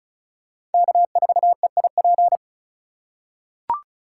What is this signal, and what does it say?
Morse code, heard as a keyed tone, about 34 wpm, 700 Hz. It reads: K4EIP
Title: USA Call Signs - 5 Characters - 3x Character Spacing - 35wpm